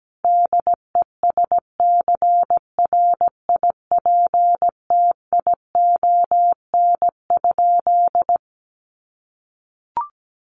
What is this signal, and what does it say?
Morse code, heard as a keyed tone, about 17 wpm, 700 Hz. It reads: DESCRIPTION?
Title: Sentences from Top 1300 Words 17wpm